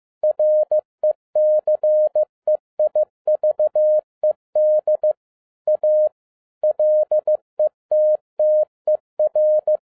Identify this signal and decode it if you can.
Morse code, heard as a keyed tone, about 15 wpm, 610 Hz. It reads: RECEIVED A LETTER
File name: 2011-10-mp3 / n07-A DOT 15wpm.mp3